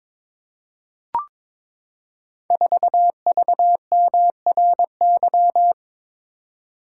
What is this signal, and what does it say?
Morse code, heard as a keyed tone, about 22 wpm, 690 Hz. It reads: 4VMRY
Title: Code Groups - 5 Letter:Numbers - 22wpm